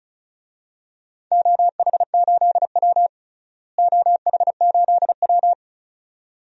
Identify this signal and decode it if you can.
Morse code, heard as a keyed tone, about 34 wpm, 700 Hz. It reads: OH8W OH8W
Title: DXCC-100 Call Signs - 4 Characters - Repeated 2x - 35wpm